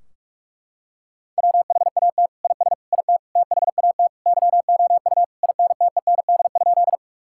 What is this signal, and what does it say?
Morse code, heard as a keyed tone, about 41 wpm, 700 Hz. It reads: WHAT IS IT THAT YOU INTEND?
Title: Sentences from Top 1600 Words 45wpm